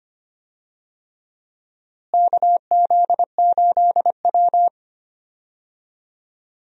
Morse code, heard as a keyed tone, about 25 wpm, 700 Hz. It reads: KZ8W
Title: USA Call Signs - 4 Characters - Repeated 3x - 25wpm